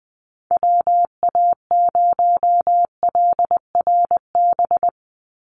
Morse code, heard as a keyed tone, about 20 wpm, 700 Hz. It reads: WA0LRB
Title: USA Call Signs 20wpm